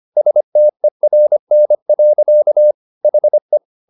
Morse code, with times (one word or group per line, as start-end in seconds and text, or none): 0.2-2.7 STERN.
3.0-3.6 HE